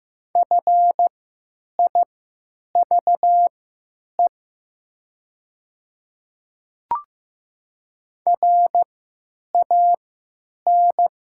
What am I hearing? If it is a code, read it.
Morse code, heard as a keyed tone, about 15 wpm, spaced as for 9 wpm, 700 Hz. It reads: FIVE RAN
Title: 3x Character Spacing - Top 500 Words - 15wpm